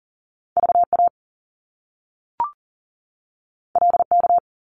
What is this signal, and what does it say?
Morse code, heard as a keyed tone, about 40 wpm, 700 Hz. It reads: VA LK
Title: Top Two Letter Combinations in Words 40wpm